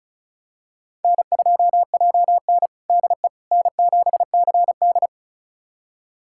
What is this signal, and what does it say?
Morse code, heard as a keyed tone, about 36 wpm, 700 Hz. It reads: N2JN DE N7CD
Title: USA Callsign DE Callsign - 4 Characters - 35wpm